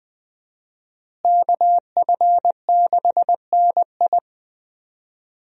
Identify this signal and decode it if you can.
Morse code, heard as a keyed tone, about 20 wpm, 700 Hz. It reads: KF6NI
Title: USA Call Signs - 5 Characters - Repeated 3x - 20wpm